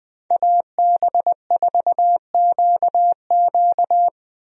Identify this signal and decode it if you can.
Morse code, heard as a keyed tone, about 20 wpm, 700 Hz. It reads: AB4QQ
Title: USA Call Signs - 5 Characters - Repeated 2x - 20wpm